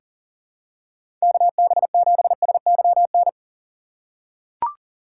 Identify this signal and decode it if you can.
Morse code, heard as a keyed tone, about 37 wpm, 700 Hz. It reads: KB7SYN
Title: USA Call Signs - 4x Character Spacing - 40wpm